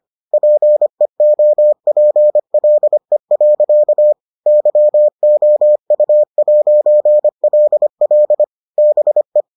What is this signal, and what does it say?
Morse code, heard as a keyed tone, about 25 wpm, 600 Hz. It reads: PEOPLE. YOU'LL BE